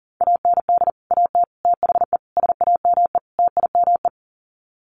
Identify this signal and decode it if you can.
Morse code, heard as a keyed tone, about 40 wpm, 700 Hz. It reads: AND AT THE SAME TIME
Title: Sets of 5 Words 40wpm